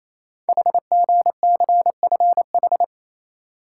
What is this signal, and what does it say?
Morse code, heard as a keyed tone, about 27 wpm, 700 Hz. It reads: HGCFH
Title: Code Groups - 5 Letters - 28wpm